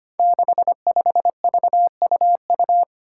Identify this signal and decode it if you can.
Morse code, heard as a keyed tone, about 25 wpm, 700 Hz. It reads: 65VUU
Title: Five Challenging Letter-Number Combinations 25wpm